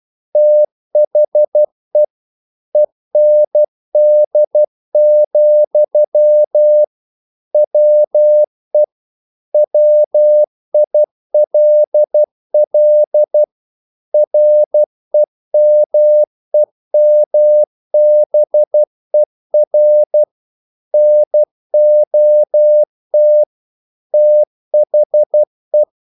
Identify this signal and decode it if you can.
Morse code, heard as a keyed tone, about 12 wpm, 600 Hz. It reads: THE END, WE WILL REMEMBER NOT THE
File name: CW / Practice / EnglishQuotations0112